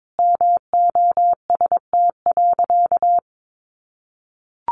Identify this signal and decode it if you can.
Morse code, heard as a keyed tone, about 22 wpm, 700 Hz. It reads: MOST.
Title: Sentences from Top 1400 Words 22wpm